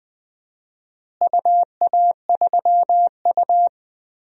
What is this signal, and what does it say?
Morse code, heard as a keyed tone, about 21 wpm, 700 Hz. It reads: UA3U
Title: DXCC-100 Call Signs - 4 Characters - Repeated 2x - 20wpm